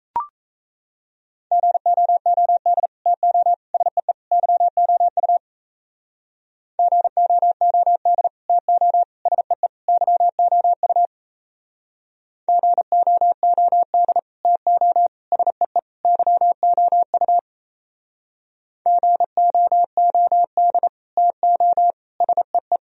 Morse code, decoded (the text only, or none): GOOD TO SEE YOU GOOD TO SEE YOU GOOD TO SEE YOU GOOD TO SEE